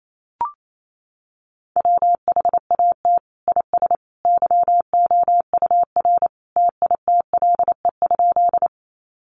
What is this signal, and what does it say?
Morse code, heard as a keyed tone, about 28 wpm, 700 Hz. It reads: WHAT IS YOUR TITLE?
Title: Sentences from Top 1200 Words 28wpm